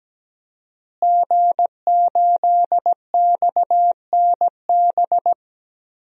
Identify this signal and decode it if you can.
Morse code, heard as a keyed tone, about 17 wpm, 700 Hz. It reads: G8XNB
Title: DXCC-100 Call Signs - 5 Characters - Repeated 2x - 17wpm